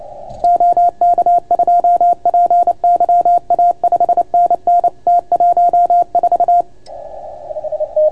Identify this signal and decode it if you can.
Morse code, heard as a keyed tone, about 29 wpm, 690 Hz. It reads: OK2PYA 5NN T14